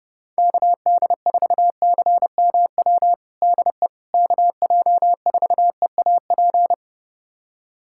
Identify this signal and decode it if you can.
Morse code, heard as a keyed tone, about 30 wpm, 700 Hz. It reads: KD4CMW DE KJ4EAP
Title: USA Callsign DE Callsign 30wpm